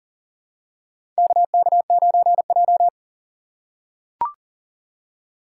Morse code, heard as a keyed tone, about 40 wpm, 700 Hz. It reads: KK9J
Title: USA Call Signs - 4 Characters - 3x Character Spacing - 40wpm